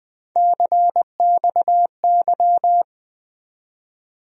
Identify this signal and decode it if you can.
Morse code, heard as a keyed tone, about 20 wpm, 700 Hz. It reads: CXY
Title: USA Call Sign Postfixes - 3 Letters 20wpm